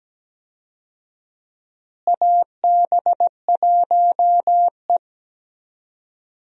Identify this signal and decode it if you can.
Morse code, heard as a keyed tone, about 17 wpm, 700 Hz. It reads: AB1E
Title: USA Call Signs - 4 Characters - Repeated 3x - 17wpm